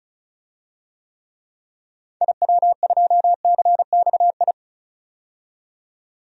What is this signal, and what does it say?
Morse code, heard as a keyed tone, about 34 wpm, 700 Hz. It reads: IW2CXI
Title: DXCC-100 Call Signs - 6 Characters - 35wpm